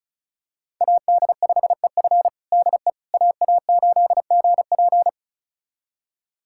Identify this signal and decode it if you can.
Morse code, heard as a keyed tone, about 36 wpm, 700 Hz. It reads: AD5EF DE AA8GP
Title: USA Callsign DE Callsign - 5 Characters - 35wpm